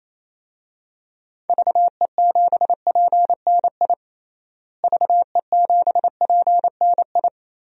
Morse code, heard as a keyed tone, about 28 wpm, 700 Hz. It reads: VE7PNI VE7PNI
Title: DXCC-100 Call Signs - 6 Characters - Repeated 3x - 28wpm